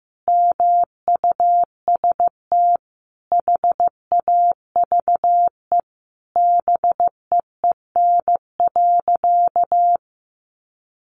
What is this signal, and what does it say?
Morse code, heard as a keyed tone, about 15 wpm, 700 Hz. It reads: MUST HAVE BEEN.
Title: Sentences from Top 1000 Words 15wpm